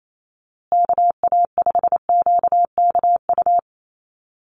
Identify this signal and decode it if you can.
Morse code, heard as a keyed tone, about 28 wpm, 700 Hz. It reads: KA5QKU
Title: USA Call Signs Repeated 2x 28wpm